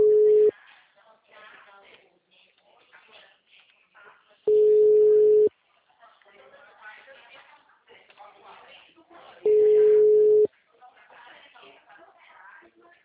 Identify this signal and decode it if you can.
Ringback tone (Europe, 425 Hz), with speech.